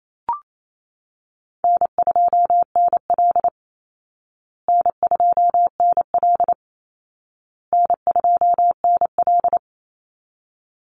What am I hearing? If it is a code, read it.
Morse code, heard as a keyed tone, about 28 wpm, 700 Hz. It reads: N2NL N2NL N2NL
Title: Easy Call Signs Repeated 3x - 28wpm